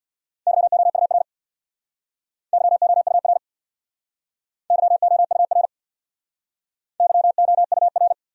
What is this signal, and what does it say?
Morse code, heard as a keyed tone, about 60 wpm, 700 Hz. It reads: YOUR YOUR YOUR YOUR
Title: Speed-Racing Top 300 Words 50wpm